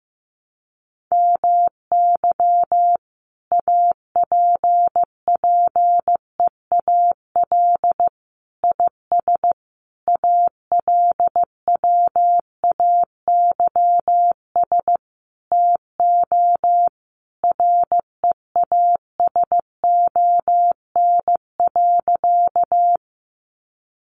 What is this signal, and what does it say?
Morse code, heard as a keyed tone, about 15 wpm, 700 Hz. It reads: MY APPEAL IS ALWAYS TO REASON.
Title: Sentences from Top 1300 Words 15wpm